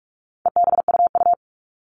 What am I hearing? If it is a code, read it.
Morse code, heard as a keyed tone, about 44 wpm, 700 Hz. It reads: EBUU
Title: All Cut Numbers - 4-Digit 45wpm